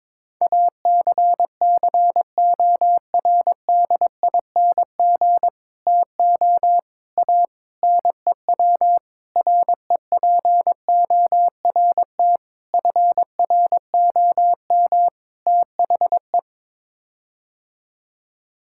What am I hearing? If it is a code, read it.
Morse code, heard as a keyed tone, about 22 wpm, 700 Hz. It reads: ACCORDING TO A NEW REPORT FROM THE
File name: practice / 22wpm / Sets of 7 Words - 22wpm